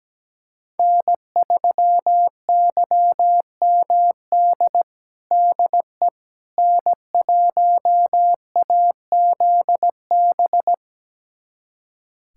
Morse code, heard as a keyed tone, about 17 wpm, 700 Hz. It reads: N3YMD DE N1AZB
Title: USA Callsign DE Callsign - 5 Characters - 17wpm